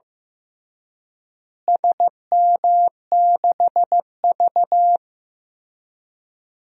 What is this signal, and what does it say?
Morse code, heard as a keyed tone, about 15 wpm, 700 Hz. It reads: SM6V
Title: DXCC-100 Call Signs - 4 Characters - Repeated 3x - 15wpm